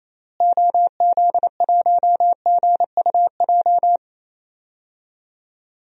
Morse code, heard as a keyed tone, about 28 wpm, 700 Hz. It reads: OZ1GUJ
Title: DXCC-100 Call Signs - 6 Characters - Repeated 3x - 28wpm